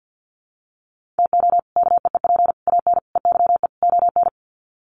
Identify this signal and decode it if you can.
Morse code, heard as a keyed tone, about 50 wpm, 700 Hz. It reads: TO KEEP AN EYE ON
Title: Sets of 5 Words Encore 50wpm